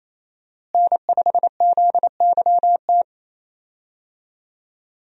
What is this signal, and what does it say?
Morse code, heard as a keyed tone, about 27 wpm, 700 Hz. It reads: N5ZYT
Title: USA Call Signs - 5 Characters - Repeated 2x - 28wpm